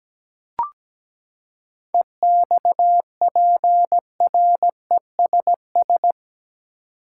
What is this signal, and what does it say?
Morse code, heard as a keyed tone, about 17 wpm, 700 Hz. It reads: EXPRESS
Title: Seven Letter Words - 17wpm